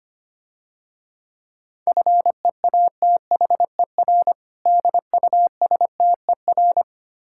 Morse code, heard as a keyed tone, about 26 wpm, 700 Hz. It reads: FEATHER DUSTER
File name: practice / 25wpm / Difficult Words - Speed-Racing with Hints - 25wpm